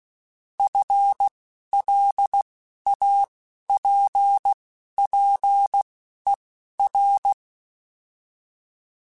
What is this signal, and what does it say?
Morse code, heard as a keyed tone, about 16 wpm, spaced as for 11 wpm, 800 Hz. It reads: FLAPPER